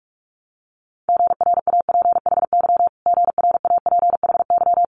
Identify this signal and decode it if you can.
Morse code, heard as a keyed tone, about 46 wpm, 700 Hz. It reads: GRAPHY GRAPHY
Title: Multi-letter Phonemes Repeated 3x 45wpm